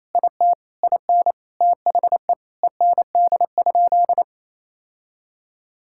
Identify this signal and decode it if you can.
Morse code, heard as a keyed tone, about 28 wpm, 700 Hz. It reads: IT IN THE END?
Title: Sentences from Top 1000 Words - Review Part 5 - 28wpm